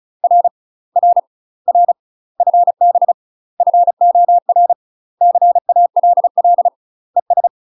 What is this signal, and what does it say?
Morse code, heard as a keyed tone, about 36 wpm, 700 Hz. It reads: R R R FB FOR CALL ES
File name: UK MASTER QSOs TRAINING / 35WPM / UK MASTER QSOs 0020